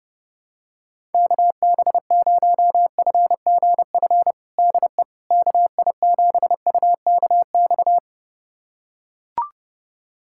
Morse code, heard as a keyed tone, about 32 wpm, 700 Hz. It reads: KB0FGF DE KI7UKX
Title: USA Callsign DE Callsign - 6 Characters - 30wpm